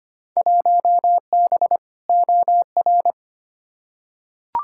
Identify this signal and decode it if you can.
Morse code, heard as a keyed tone, about 25 wpm, 700 Hz. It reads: 1B OR
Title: Field Day Exchanges - 25wpm